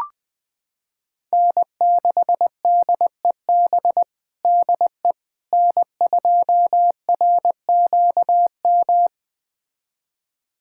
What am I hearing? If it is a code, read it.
Morse code, heard as a keyed tone, about 20 wpm, 700 Hz. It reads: N6DEB DE N2RQM
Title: USA Callsign DE Callsign - 5 Characters - 20wpm